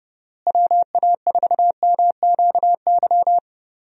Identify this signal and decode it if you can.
Morse code, heard as a keyed tone, about 30 wpm, 700 Hz. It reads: WA4MQY
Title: USA Call Signs - 6 Characters - Repeated 3x - 30wpm